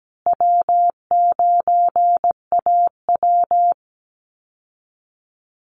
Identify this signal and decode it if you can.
Morse code, heard as a keyed tone, about 17 wpm, 700 Hz. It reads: W9AW
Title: Easy Call Signs Repeated 3x - 17wpm